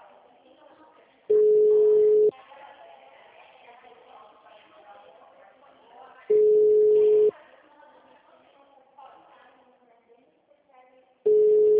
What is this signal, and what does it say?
Ringback tone (Europe, 425 Hz), with speech.
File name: out- -2022- - - .9739